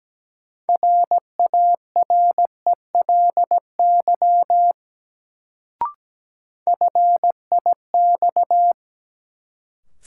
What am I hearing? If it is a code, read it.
Morse code, heard as a keyed tone, about 17 wpm, 700 Hz. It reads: RARELY FIX
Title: Sentences from Top 1400 Words - New Word Drill Alt - 17wpm